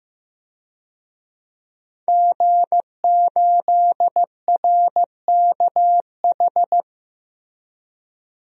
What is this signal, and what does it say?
Morse code, heard as a keyed tone, about 15 wpm, 700 Hz. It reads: G8RKH
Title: DXCC-100 Call Signs - 5 Characters - Repeated 2x - 15wpm